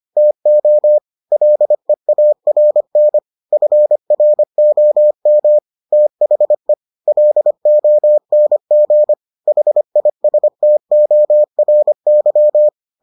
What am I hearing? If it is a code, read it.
Morse code, heard as a keyed tone, about 25 wpm, 600 Hz. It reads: TO LEARN FROM THE LONG HISTORY